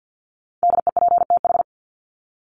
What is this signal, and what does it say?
Morse code, heard as a keyed tone, about 48 wpm, 700 Hz. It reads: DEPTH
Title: Top 1001-1250 Words 50wpm